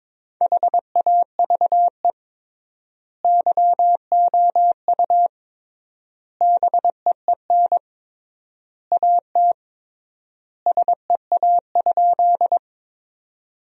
Morse code, heard as a keyed tone, about 22 wpm, 700 Hz. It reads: HAVE YOU BEEN AT SEA?
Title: 3x Word-Spacing - Sentences from Top 300 Words - 22wpm